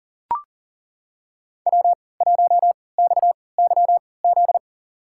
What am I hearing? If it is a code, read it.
Morse code, heard as a keyed tone, about 40 wpm, spaced as for 23 wpm, 700 Hz. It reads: W1XYZ
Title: USA Call Signs - 5 Characters - 3x Character Spacing - 40wpm